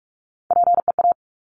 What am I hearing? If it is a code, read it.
Morse code, heard as a keyed tone, about 46 wpm, 700 Hz. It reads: PEA
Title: Top Three Letter Combinations in Words 45wpm